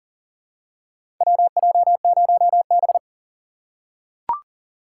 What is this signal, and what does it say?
Morse code, heard as a keyed tone, about 38 wpm, 700 Hz. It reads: WJ0B
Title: USA Call Signs - 4 Characters - 3x Character Spacing - 40wpm